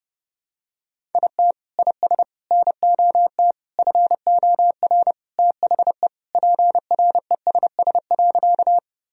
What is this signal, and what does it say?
Morse code, heard as a keyed tone, about 30 wpm, 700 Hz. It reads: IT IS NOT FOR THE PRESS.
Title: Sentences from Top 1000 Words - Review Part 1 - 30wpm